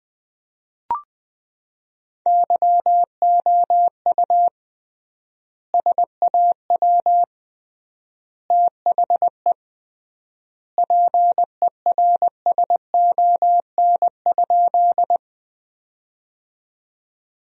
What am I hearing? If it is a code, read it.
Morse code, heard as a keyed tone, about 20 wpm, 700 Hz. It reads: YOU SAW THE PERSON?
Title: 3x Word-Spacing - Sentences from Top 400 Words - 20wpm